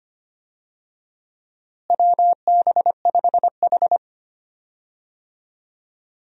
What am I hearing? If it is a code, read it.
Morse code, heard as a keyed tone, about 25 wpm, 700 Hz. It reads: WB5H
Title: USA Call Signs - 4 Characters - Repeated 3x - 25wpm